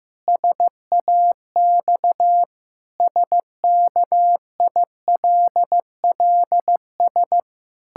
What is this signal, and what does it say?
Morse code, heard as a keyed tone, about 15 wpm, 700 Hz. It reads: SAX SKILLS
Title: Pangram Challenge v2 - 15wpm